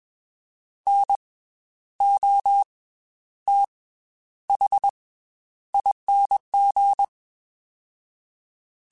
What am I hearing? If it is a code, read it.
Morse code, heard as a keyed tone, about 21 wpm, 800 Hz. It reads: N O T H ING